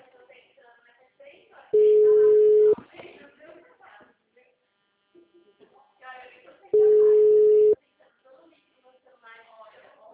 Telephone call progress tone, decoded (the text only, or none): ringback tone